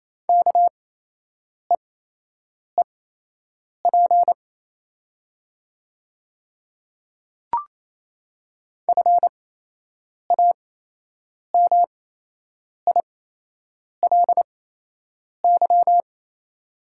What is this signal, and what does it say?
Morse code, heard as a keyed tone, about 27 wpm, spaced as for 8 wpm, 690 Hz. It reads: KEEP FAMILY
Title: 8x Character Spacing - Top 500 Words - 28wpm